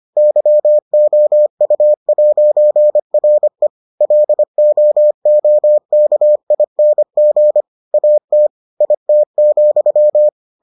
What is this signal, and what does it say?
Morse code, heard as a keyed tone, about 25 wpm, 600 Hz. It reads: YOU'RE LOOKING AT IT,